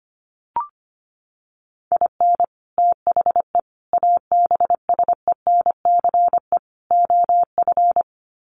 Morse code, heard as a keyed tone, about 25 wpm, 700 Hz. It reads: IN THE ABSENCE OF
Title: Sets of 4 Words 25wpm